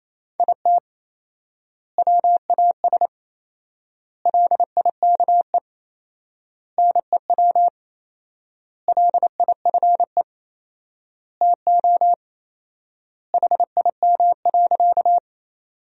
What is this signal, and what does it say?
Morse code, heard as a keyed tone, about 27 wpm, 700 Hz. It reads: IT WAS LIKE NEW LIFE TO HIM.
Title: 4x Word-Spacing - Sentences from Top 300 Words - 28wpm